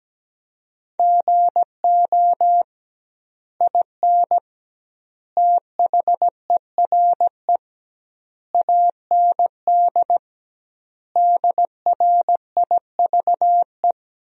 Morse code, heard as a keyed tone, about 17 wpm, 700 Hz. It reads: GO IN THERE AND DRIVE
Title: 2x Word-Spacing - Sentences from Top 500 Words - 17wpm